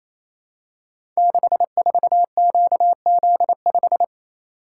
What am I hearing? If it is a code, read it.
Morse code, heard as a keyed tone, about 28 wpm, 700 Hz. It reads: 64QZ5
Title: Code Groups - 5 Letter:Numbers - 28wpm